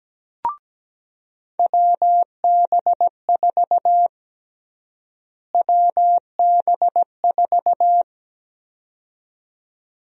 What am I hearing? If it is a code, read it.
Morse code, heard as a keyed tone, about 17 wpm, 700 Hz. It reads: WB4 WB4
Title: USA Call Sign Prefixes - Repeated 2x - 17wpm